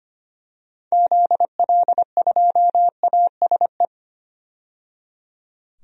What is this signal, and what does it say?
Morse code, heard as a keyed tone, about 25 wpm, 700 Hz. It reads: ZL2ASE
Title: DXCC-100 Call Signs - 6 Characters - 25wpm